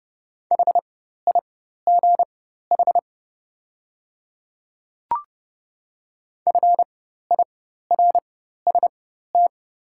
Morse code, heard as a keyed tone, about 30 wpm, spaced as for 14 wpm, 700 Hz. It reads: HIGH FIRST
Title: 4x Character Spacing - Top 200 Words - 30wpm